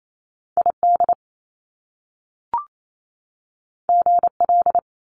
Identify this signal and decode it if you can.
Morse code, heard as a keyed tone, about 28 wpm, 700 Hz. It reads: ID GL
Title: Top Two Letter Combinations in Words 28wpm